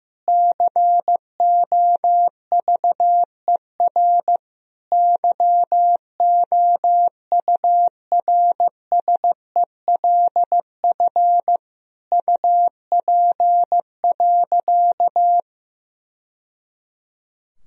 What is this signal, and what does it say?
Morse code, heard as a keyed tone, about 15 wpm, 700 Hz. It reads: COVER YOURSELF UP.
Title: Sentences from Top 1000 Words - Review Part 1 - 15wpm